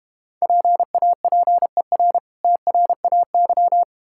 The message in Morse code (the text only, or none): PAPER TRAY